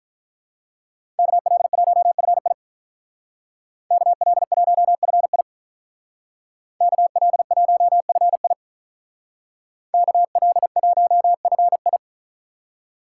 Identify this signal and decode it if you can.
Morse code, heard as a keyed tone, about 46 wpm, 700 Hz. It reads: KL1FI KL1FI KL1FI KL1FI